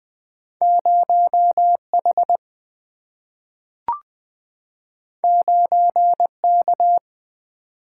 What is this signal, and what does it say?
Morse code, heard as a keyed tone, about 20 wpm, 700 Hz. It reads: 0H 9K
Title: Single-Letter Single-Number 20wpm